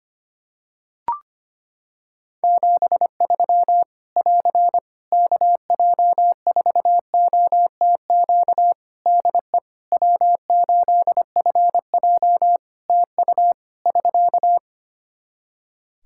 Morse code, heard as a keyed tone, about 25 wpm, 700 Hz. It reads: 73 <AR> KJ4OTQ DE W8FJ TU <SK>
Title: Practice QSO - Exchange 4 - Ending - 25wpm